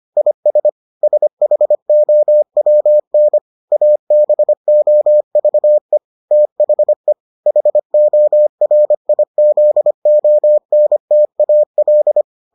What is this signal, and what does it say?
Morse code, heard as a keyed tone, about 25 wpm, 600 Hz. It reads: IS SHOWN ABOVE THE HORIZONTAL